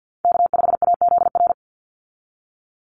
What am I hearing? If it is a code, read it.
Morse code, heard as a keyed tone, about 50 wpm, 700 Hz. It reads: K5AZR
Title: Call Signs Repeated 3x - 50wpm